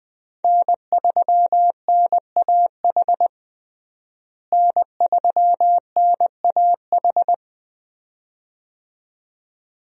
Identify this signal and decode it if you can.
Morse code, heard as a keyed tone, about 20 wpm, 700 Hz. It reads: N3NAH N3NAH
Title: USA Call Signs - 5 Characters - Repeated 2x - 20wpm